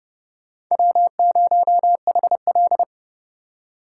Morse code, heard as a keyed tone, about 27 wpm, 700 Hz. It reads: W0HL